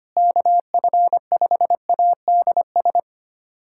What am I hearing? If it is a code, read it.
Morse code, heard as a keyed tone, about 25 wpm, 700 Hz. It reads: KF5ADS
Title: USA Callsign DE Callsign 25wpm